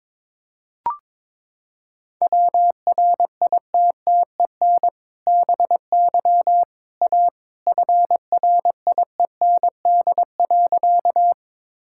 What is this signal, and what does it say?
Morse code, heard as a keyed tone, about 22 wpm, 700 Hz. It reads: WRITTEN BY A FRIEND.
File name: practice / 22wpm / Sentences with 2-Word Phrases - 22wpm